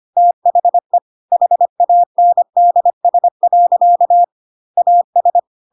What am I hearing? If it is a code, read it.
Morse code, heard as a keyed tone, about 25 wpm, 700 Hz. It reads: THE HANDS. AS
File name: Pinocchio / Pinocchio ch3 25 25 ik1hge